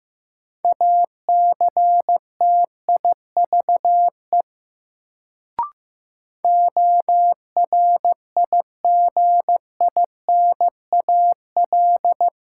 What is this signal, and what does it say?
Morse code, heard as a keyed tone, about 15 wpm, 700 Hz. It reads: ACTIVE ORIGINAL
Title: Sentences from Top 900 Words - New Word Drill - 15wpm